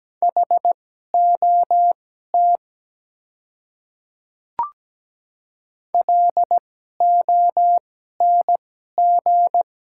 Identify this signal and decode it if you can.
Morse code, heard as a keyed tone, about 17 wpm, spaced as for 12 wpm, 700 Hz. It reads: HOT LONG
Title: 2x Character Spacing - Top 200 Words - 17wpm